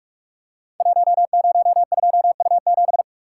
Morse code, heard as a keyed tone, about 44 wpm, 700 Hz. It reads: 102U7
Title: Code Groups - 5 Letter:Numbers - 45wpm